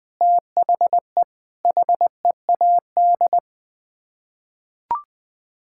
Morse code, heard as a keyed tone, about 20 wpm, 700 Hz. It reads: THE HEAD
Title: Sets of 6 Words - 20wpm